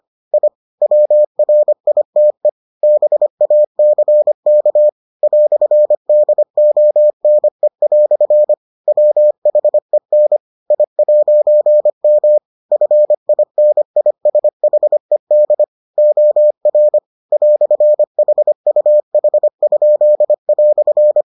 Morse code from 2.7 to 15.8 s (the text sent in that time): BACK "DONE" WHEN I'M FINISHED